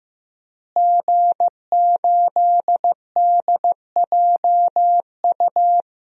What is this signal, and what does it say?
Morse code, heard as a keyed tone, about 16 wpm, 700 Hz. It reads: G8DJU
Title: DXCC-100 Call Signs - 5 Characters - Repeated 3x - 15wpm